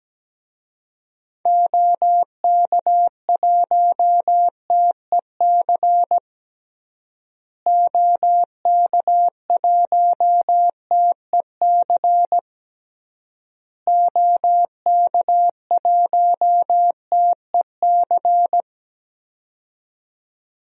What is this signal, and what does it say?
Morse code, heard as a keyed tone, about 17 wpm, 700 Hz. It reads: OK1TEC OK1TEC OK1TEC
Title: DXCC-100 Call Signs - 6 Characters - Repeated 3x - 17wpm